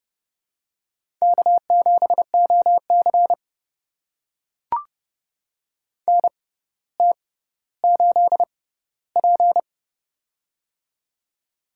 Morse code, heard as a keyed tone, about 29 wpm, spaced as for 15 wpm, 700 Hz. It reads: K7OC NT8P
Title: USA Call Signs - 4 Characters - 6x Character Spacing - 30wpm